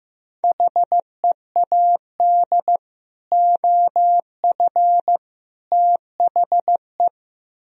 Morse code, heard as a keyed tone, about 15 wpm, 700 Hz. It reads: HEAD OF THE
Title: Speed-Racing - Sets of 3 Words 15wpm